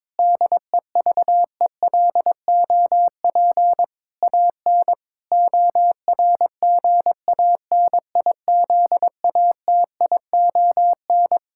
Morse code, heard as a keyed tone, about 22 wpm, 700 Hz. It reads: DEVELOP AN ORGANIZATION
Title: Sentences from Top 1000 Words - Review Part 5 - 22wpm